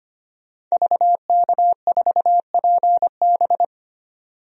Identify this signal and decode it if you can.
Morse code, heard as a keyed tone, about 25 wpm, 700 Hz. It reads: VK4PB
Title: DXCC-100 Call Signs - 5 Characters - 25wpm